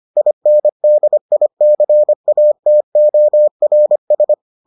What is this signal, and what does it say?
Morse code, heard as a keyed tone, about 25 wpm, 600 Hz. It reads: INDICATORS